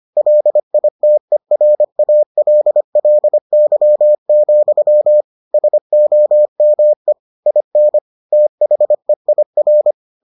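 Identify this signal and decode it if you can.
Morse code, heard as a keyed tone, about 25 wpm, 600 Hz. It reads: LITERALLY, SOME IN THEIR
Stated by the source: Morse edition of the Foundations of Amateur Radio episodes.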